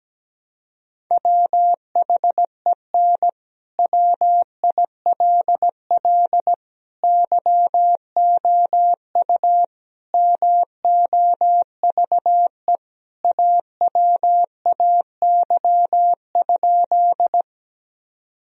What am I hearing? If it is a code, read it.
Morse code, heard as a keyed tone, about 17 wpm, 700 Hz. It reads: WHEN WILL YOU MOVE AWAY?
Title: Sentences from Top 1000 Words - Review Part 1 - 17wpm